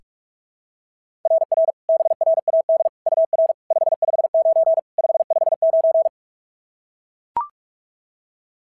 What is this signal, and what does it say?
Morse code, heard as a keyed tone, about 44 wpm, 690 Hz. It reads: RR BRAD UR 559 559
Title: SOTA QSO Practice 45wpm